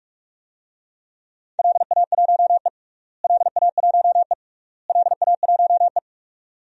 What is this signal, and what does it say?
Morse code, heard as a keyed tone, about 46 wpm, 700 Hz. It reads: LA1E LA1E LA1E
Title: DXCC-100 Call Signs - 4 Characters - Repeated 3x - 45wpm